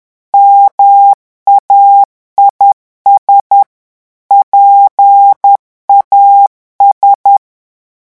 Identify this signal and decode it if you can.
Morse code, heard as a keyed tone, about 10 wpm, 800 Hz. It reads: MAIS PAS
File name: morse mp3 / revision francais 10 mots minute